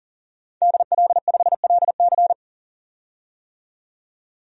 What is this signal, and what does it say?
Morse code, heard as a keyed tone, about 40 wpm, 700 Hz. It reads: DL5LC
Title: DXCC-100 Call Signs - 5 Characters - 40wpm